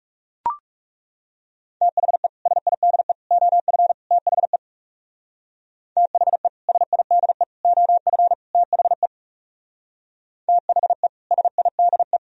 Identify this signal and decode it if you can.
Morse code, heard as a keyed tone, about 36 wpm, 700 Hz. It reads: THE SIDE OF THE THE SIDE OF THE THE SIDE
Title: Speed-Racing - Sets of 4 Words 30wpm